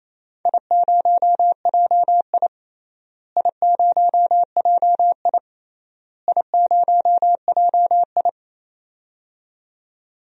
Morse code, heard as a keyed tone, about 27 wpm, 700 Hz. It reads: I0JI I0JI I0JI
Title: DXCC-100 Call Signs - 4 Characters - Repeated 3x - 28wpm